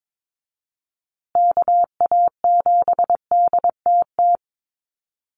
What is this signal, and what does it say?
Morse code, heard as a keyed tone, about 22 wpm, 700 Hz. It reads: KA7DTT
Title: USA Call Signs Repeated 2x 22wpm